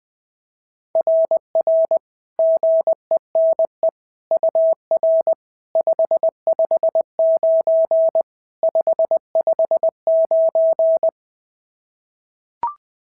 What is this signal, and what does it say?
Morse code, heard as a keyed tone, about 20 wpm, 690 Hz. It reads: RR GENE UR 559 559
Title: SOTA QSO Practice 20wpm